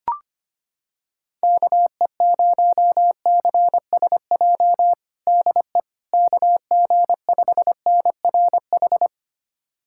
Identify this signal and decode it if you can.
Morse code, heard as a keyed tone, about 25 wpm, 700 Hz. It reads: KE0CSJ DE KG5NRH
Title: USA Callsign DE Callsign - 6 Characters - 25wpm